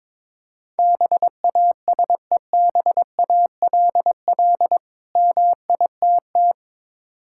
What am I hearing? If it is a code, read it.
Morse code, heard as a keyed tone, about 22 wpm, 690 Hz. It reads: BASEBALL MITT
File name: Difficult Words with Hints - 22wpm